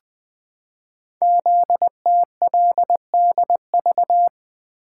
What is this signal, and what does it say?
Morse code, heard as a keyed tone, about 20 wpm, 700 Hz. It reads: ZTLDV
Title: Code Groups - 5 Letters - 20wpm